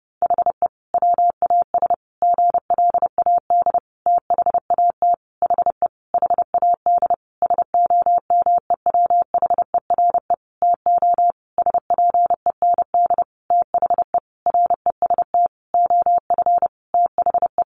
Morse code, heard as a keyed tone, about 30 wpm, 700 Hz. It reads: HE WAS GLAD THAT HE HAD SOMEWHERE TO SPEND THE REST OF THE
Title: Sentences from Top 1400 Words 30wpm